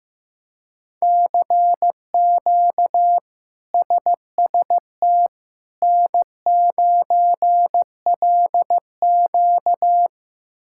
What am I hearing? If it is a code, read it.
Morse code, heard as a keyed tone, about 15 wpm, 700 Hz. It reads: CQ SST N9LQ
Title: K1USN SST QSO Practice - 15wpm